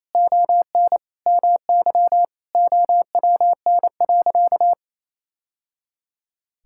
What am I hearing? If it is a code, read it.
Morse code, heard as a keyed tone, about 28 wpm, 700 Hz. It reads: ON MY OWN.
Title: Sentences from Top 1000 Words - Review Part 6 - 28wpm